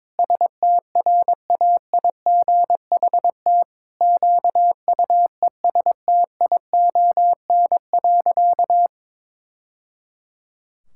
Morse code, heard as a keyed tone, about 22 wpm, 700 Hz. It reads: STRAIGHT QUESTION.
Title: Sentences from Top 1000 Words - Review Part 6 - 22wpm